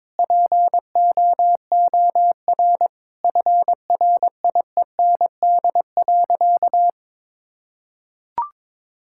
Morse code, heard as a keyed tone, about 21 wpm, 700 Hz. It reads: POOR FRIEND.
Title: Sentences from Top 1000 Words - Review Part 5 - 22wpm